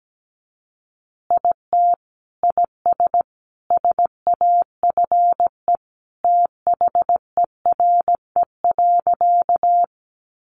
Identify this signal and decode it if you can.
Morse code, heard as a keyed tone, about 17 wpm, 700 Hz. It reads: IT IS SAFE THERE.
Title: Easy Phrases 17wpm